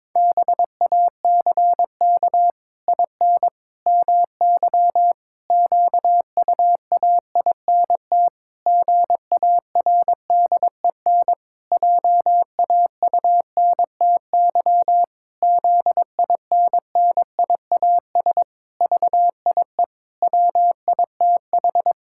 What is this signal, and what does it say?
Morse code, heard as a keyed tone, about 22 wpm, 700 Hz. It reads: BACK IN MY QUAINT GARDEN JAUNTY ZINNIAS VIE WITH
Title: Pangram Challenge v2 - 22wpm